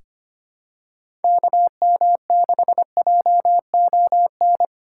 Morse code, heard as a keyed tone, about 25 wpm, 700 Hz. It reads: KM6JON
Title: USA Call Signs - 6 Characters - 2x Character Spacing - 25wpm